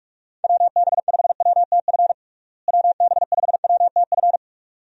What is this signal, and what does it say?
Morse code, heard as a keyed tone, about 42 wpm, 700 Hz. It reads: WB5WTF WB5WTF
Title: USA Call Signs - 6 Characters - Repeated 2x - 45wpm